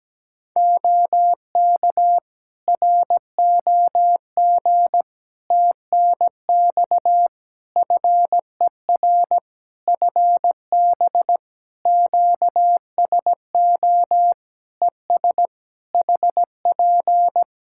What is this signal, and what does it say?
Morse code, heard as a keyed tone, about 17 wpm, 700 Hz. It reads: OK ROG TNX FER FB QSO ES HP
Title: Practice QSO - Exchange 4 - Ending - 17wpm